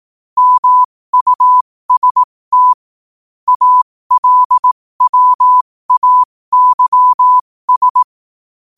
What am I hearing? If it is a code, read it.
Morse code, heard as a keyed tone, about 18 wpm, spaced as for 15 wpm, 1000 Hz. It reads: MUST ALWAYS
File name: qotdfiles / 15wpm